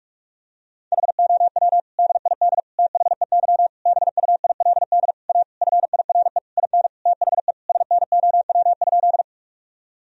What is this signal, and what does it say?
Morse code, heard as a keyed tone, about 41 wpm, 700 Hz. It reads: HOW DID THEY BUILD A FIRE IN THE SNOW?
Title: Sentences from Top 1000 Words - Review Part 3 - 45wpm